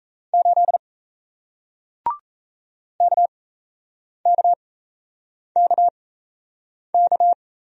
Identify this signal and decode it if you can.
Morse code, heard as a keyed tone, about 38 wpm, spaced as for 8 wpm, 700 Hz. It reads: 8 KKKK